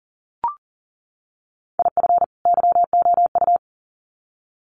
Morse code, heard as a keyed tone, about 40 wpm, 700 Hz. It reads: IF YOU
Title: Sets of 2 Words 40wpm